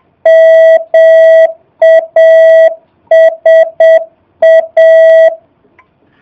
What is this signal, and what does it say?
Morse code, heard as a keyed tone, about 7 wpm, 650 Hz. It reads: MASA
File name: jh8cbh / 2019cw / SP05- w